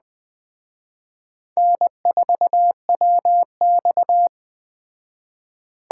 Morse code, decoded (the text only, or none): N4WX